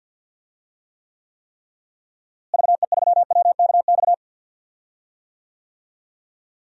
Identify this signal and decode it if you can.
Morse code, heard as a keyed tone, about 46 wpm, 700 Hz. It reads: VE3WKX
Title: DXCC-100 Call Signs - 6 Characters - Repeated 3x - 50wpm